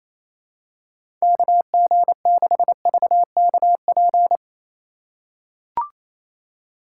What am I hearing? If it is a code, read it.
Morse code, heard as a keyed tone, about 28 wpm, 700 Hz. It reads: KG6VKP
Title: USA Call Signs - 3x Character Spacing - 28wpm